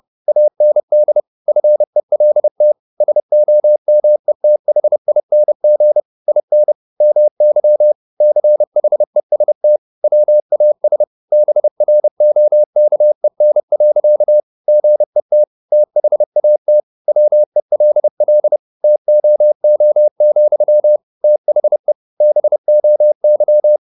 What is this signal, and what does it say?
Morse code, heard as a keyed tone, about 30 wpm, 600 Hz. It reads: AND FELT SOMETHING IN MY CHEST WAS BROKEN. GET THAT WELL TOO, THE BOY